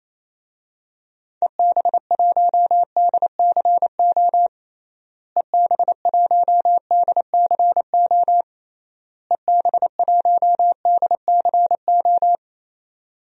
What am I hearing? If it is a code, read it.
Morse code, heard as a keyed tone, about 28 wpm, 700 Hz. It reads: EB1DCO EB1DCO EB1DCO
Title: DXCC-100 Call Signs - 6 Characters - Repeated 3x - 28wpm